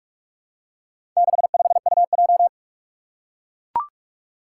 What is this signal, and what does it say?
Morse code, heard as a keyed tone, about 44 wpm, 700 Hz. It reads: 65UJ
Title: Four Challenging Letter-Number Combinations 45wpm